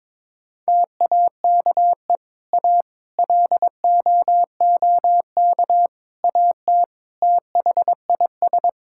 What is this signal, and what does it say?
Morse code, heard as a keyed tone, about 22 wpm, 700 Hz. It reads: TAKE A LOOK AT THIS
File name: practice / 22wpm / Sentences with 4-Word Phrases - 22wpm